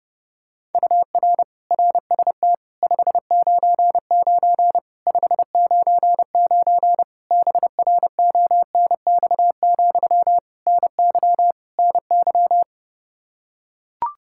Morse code, heard as a keyed tone, about 30 wpm, 700 Hz. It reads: UR RST 599 599 BRONX, NY NY
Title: Practice QSO - Exchange 1 - Beginning - 30wpm